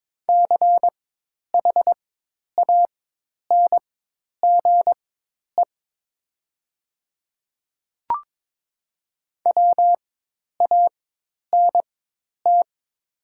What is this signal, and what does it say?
Morse code, heard as a keyed tone, about 21 wpm, spaced as for 10 wpm, 700 Hz. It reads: CHANGE WANT
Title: 4x Character Spacing - Top 200 Words - 22wpm